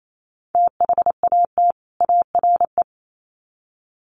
Morse code, heard as a keyed tone, about 28 wpm, 700 Hz. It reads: THAT ARE
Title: Sets of 2 Words 28wpm